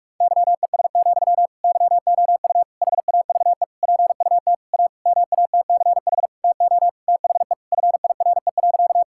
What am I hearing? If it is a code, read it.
Morse code, heard as a keyed tone, about 46 wpm, 700 Hz. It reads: YES, YOU HAVE PUT A MATCH TO THE FIRE.
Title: Sentences from Top 1600 Words 45wpm